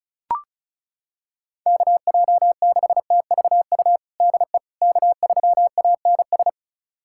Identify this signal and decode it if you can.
Morse code, heard as a keyed tone, about 36 wpm, 700 Hz. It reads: KJ6TVU DE K3ANS
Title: USA Callsign DE Callsign 35wpm